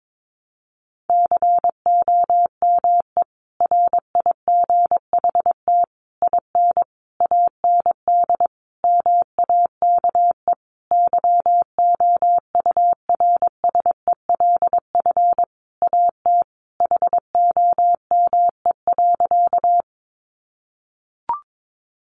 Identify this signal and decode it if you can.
Morse code, heard as a keyed tone, about 22 wpm, 700 Hz. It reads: COME RIGHT IN AND MAKE YOURSELF AT HOME.
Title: Sentences from Top 600 Words 22wpm